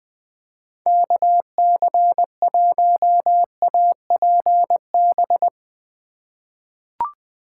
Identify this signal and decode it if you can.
Morse code, heard as a keyed tone, about 21 wpm, 700 Hz. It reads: KC1APB
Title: USA Call Signs - 2x Character Spacing - 20wpm